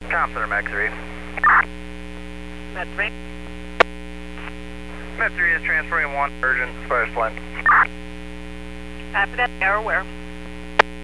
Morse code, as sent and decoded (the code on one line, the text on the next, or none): none